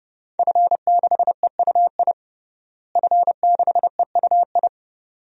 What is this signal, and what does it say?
Morse code, heard as a keyed tone, about 32 wpm, 700 Hz. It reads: F6EUI F6EUI
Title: DXCC-100 Call Signs - 5 Characters - Repeated 2x - 30wpm